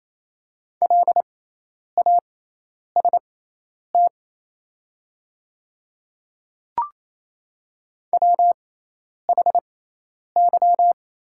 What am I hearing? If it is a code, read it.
Morse code, heard as a keyed tone, about 28 wpm, spaced as for 10 wpm, 700 Hz. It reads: LAST WHY
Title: 6x Character Spacing - Top 300 Words - 28wpm